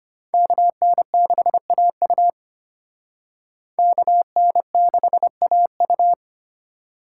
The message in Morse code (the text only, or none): KN6AU KN6AU